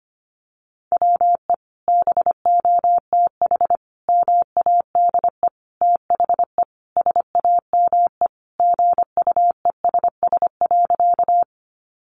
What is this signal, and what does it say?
Morse code, heard as a keyed tone, about 25 wpm, 700 Hz. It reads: WE BOTH MADE THE SAME GUESS.
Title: Sentences from Top 1500 Words 25wpm